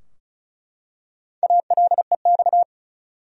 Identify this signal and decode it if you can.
Morse code, heard as a keyed tone, about 36 wpm, 700 Hz. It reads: ALEX
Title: USA Common Names - Max 4 Letters - 35wpm